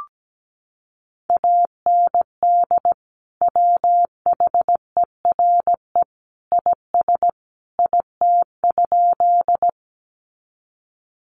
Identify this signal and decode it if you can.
Morse code, heard as a keyed tone, about 17 wpm, 700 Hz. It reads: AND WHERE IS IT?
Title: Sentences from Top 200 Words 17wpm